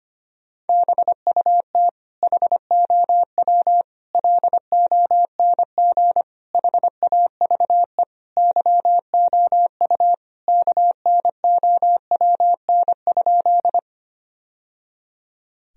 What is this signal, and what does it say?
Morse code, heard as a keyed tone, about 25 wpm, 700 Hz. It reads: BUT HOW LONG HAVE YOU KNOWN?
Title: Sentences from Top 1000 Words - Review Part 3 - 25wpm